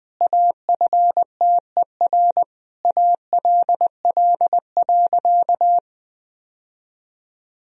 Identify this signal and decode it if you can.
Morse code, heard as a keyed tone, about 20 wpm, 700 Hz. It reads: AFTER ALL.
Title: Sentences from Top 1000 Words - Review Part 2 - 20wpm